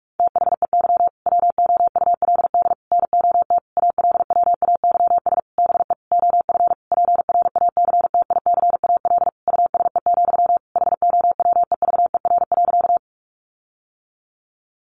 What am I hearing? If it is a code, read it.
Morse code, heard as a keyed tone, about 44 wpm, 700 Hz. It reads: THEY WOULD NOT ALWAYS BE OF PRACTICAL USE, HOWEVER.
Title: Sentences from Top 1400 Words 45wpm